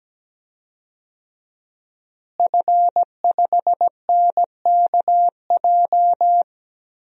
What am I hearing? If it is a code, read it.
Morse code, heard as a keyed tone, about 17 wpm, 700 Hz. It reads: F5NKJ
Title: DXCC-100 Call Signs - 5 Characters - Repeated 3x - 17wpm